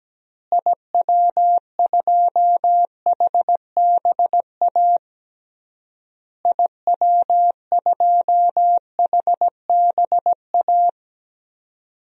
Morse code, heard as a keyed tone, about 17 wpm, 700 Hz. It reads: IW2HBA IW2HBA
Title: DXCC-100 Call Signs - 6 Characters - Repeated 3x - 17wpm